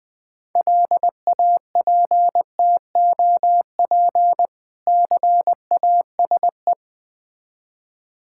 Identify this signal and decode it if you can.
Morse code, heard as a keyed tone, about 20 wpm, 700 Hz. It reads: LAPTOP CASE
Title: Difficult Words with Hints - 20wpm